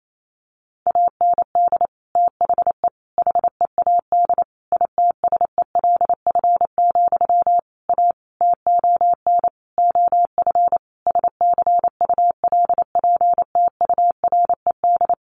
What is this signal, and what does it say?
Morse code, heard as a keyed tone, about 28 wpm, 700 Hz. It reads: AND THE HEAD ITSELF, A TON OF SCULPTURED
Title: A Sound Of Thunder 28wpm